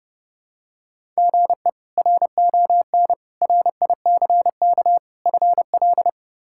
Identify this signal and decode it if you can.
Morse code, heard as a keyed tone, about 29 wpm, 700 Hz. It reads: GE RON RICK FL
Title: K1USN SST QSO Practice - 30wpm